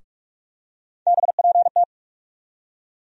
Morse code, heard as a keyed tone, about 47 wpm, 700 Hz. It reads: BPT
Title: USA Call Sign Postfixes - 3 Letters 45wpm